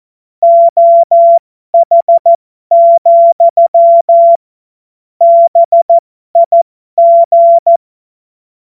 Morse code, heard as a keyed tone, about 14 wpm, spaced as for 12 wpm, 680 Hz. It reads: OH, BIG
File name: Morsekurs / Witze